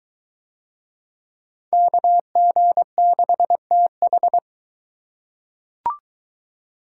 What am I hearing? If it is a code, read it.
Morse code, heard as a keyed tone, about 20 wpm, 700 Hz. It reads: KG6TH